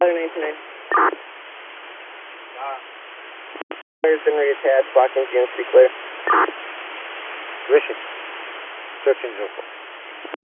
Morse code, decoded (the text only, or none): none